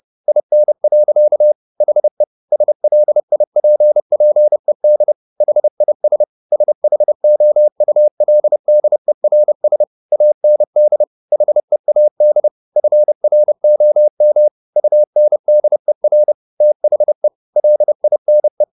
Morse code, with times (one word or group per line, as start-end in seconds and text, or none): 0.3-1.5 IN.
1.8-2.2 HE
2.5-5.1 SLIPPED
5.4-6.2 HIS
6.5-9.8 SHOULDERS
10.1-11.0 AND
11.3-12.5 HEAD
12.8-14.5 FROM
14.8-16.3 UNDER
16.6-17.3 THE
17.6-18.6 LINE